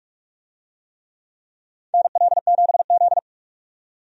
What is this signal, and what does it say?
Morse code, heard as a keyed tone, about 46 wpm, 700 Hz. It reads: NL7Z
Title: USA Call Signs - 4 Characters - 45wpm